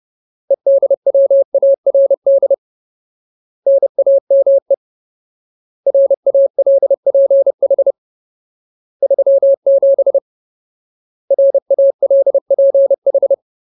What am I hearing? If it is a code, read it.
Morse code, heard as a keyed tone, about 30 wpm, 550 Hz. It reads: EDWARD NAME RALPH 37 RALPH
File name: Ultimate Word List / 30WPM Ultimate